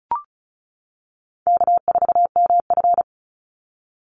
Morse code, heard as a keyed tone, about 34 wpm, 700 Hz. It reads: K4MF
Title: NA SOTA Chaser Callsigns 35wpm